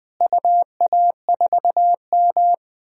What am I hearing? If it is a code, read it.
Morse code, heard as a keyed tone, about 20 wpm, 700 Hz. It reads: UA4M
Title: DXCC-100 Call Signs - 4 Characters - Repeated 2x - 20wpm